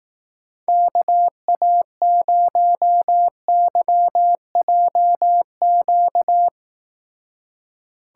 Morse code, heard as a keyed tone, about 18 wpm, 700 Hz. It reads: KA0YJQ
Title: Speed Racing - USA Call Signs 15wpm